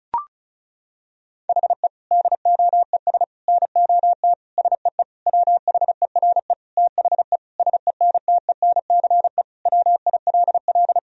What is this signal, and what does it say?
Morse code, heard as a keyed tone, about 34 wpm, 700 Hz. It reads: HE DOES NOT SEE WHERE THE SENTENCE WILL
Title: Sentences from Top 1000 Words - Review Part 1 - 35wpm